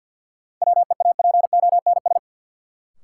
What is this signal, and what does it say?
Morse code, heard as a keyed tone, about 52 wpm, 700 Hz. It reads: WEAPONS
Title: Sets of 5 Words Encore - New Word Drill - 50wpm